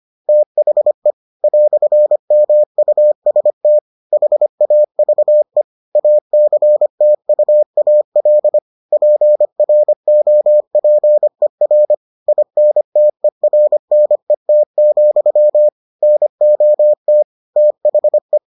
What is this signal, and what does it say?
Morse code, heard as a keyed tone, about 25 wpm, 600 Hz. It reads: THE "MUST HAVE ACTUAL PROPER INTERNET, NOT THE